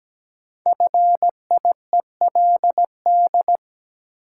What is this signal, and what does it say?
Morse code, heard as a keyed tone, about 17 wpm, 700 Hz. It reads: FIELD
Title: Sentences from Top 1000 Words - New Word Drill - 17wpm